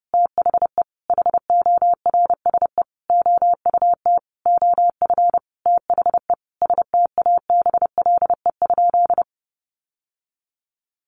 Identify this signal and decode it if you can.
Morse code, heard as a keyed tone, about 28 wpm, 700 Hz. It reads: THE HORSE OUT OF THE STABLE?
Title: Sentences from Top 1500 Words 30wpm